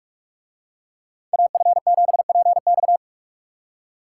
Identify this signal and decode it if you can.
Morse code, heard as a keyed tone, about 45 wpm, 700 Hz. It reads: AF7PX